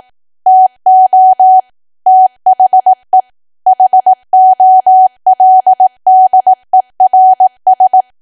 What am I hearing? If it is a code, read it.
Morse code, heard as a keyed tone, about 18 wpm, 750 Hz. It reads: TO THE HOLDERS
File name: cwa / W1AW-Level2 / 18WPM